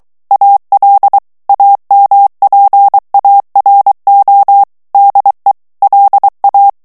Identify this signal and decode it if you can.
Morse code, heard as a keyed tone, about 23 wpm, 800 Hz. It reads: AL AMPARO DE LA
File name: morse mp3 / ESPAGNOL TEXTE EN CLAIR 2 VITESSE 22 MOTS MINUTES